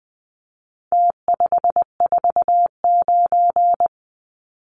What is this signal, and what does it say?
Morse code, heard as a keyed tone, about 19 wpm, 700 Hz. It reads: T549
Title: Numbers - Contest Style - 4-Digit 20wpm